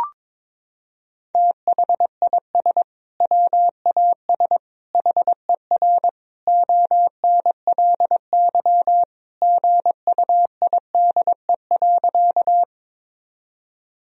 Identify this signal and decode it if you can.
Morse code, heard as a keyed tone, about 22 wpm, 700 Hz. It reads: THIS WAS HER ONLY GUIDE.
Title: Sentences from Top 1000 Words - Review Part 3 - 22wpm